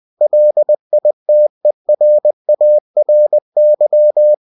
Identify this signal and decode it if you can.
Morse code, heard as a keyed tone, about 20 wpm, 600 Hz. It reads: LITERARY